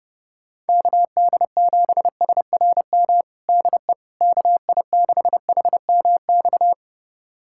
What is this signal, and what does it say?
Morse code, heard as a keyed tone, about 32 wpm, 700 Hz. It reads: KD7SRM DE KI6HMX
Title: USA Callsign DE Callsign 30wpm